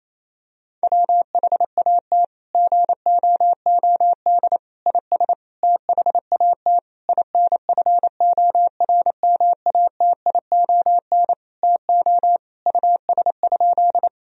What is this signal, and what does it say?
Morse code, heard as a keyed tone, about 27 wpm, 700 Hz. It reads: WHAT GOOD IS THAT INFORMATION TO US?
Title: Sentences from Top 1000 Words - Review Part 3 - 28wpm